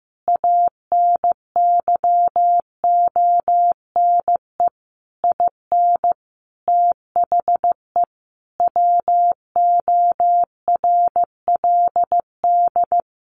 Morse code, heard as a keyed tone, about 15 wpm, 700 Hz. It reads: ANYONE IN THE WORLD
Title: Sentences from Top 900 Words 15wpm